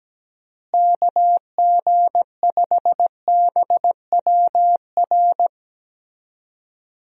Morse code, heard as a keyed tone, about 17 wpm, 700 Hz. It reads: KG5BWR
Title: USA Call Signs - 6 Characters - Repeated 3x - 17wpm